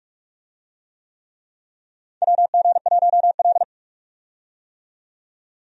Morse code, heard as a keyed tone, about 43 wpm, 700 Hz. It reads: WG1L
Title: USA Call Signs - 4 Characters - 45wpm